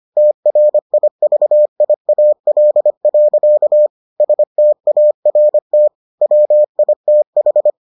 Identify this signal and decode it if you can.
Morse code, heard as a keyed tone, about 25 wpm, 600 Hz. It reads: TRIVIAL. START WITH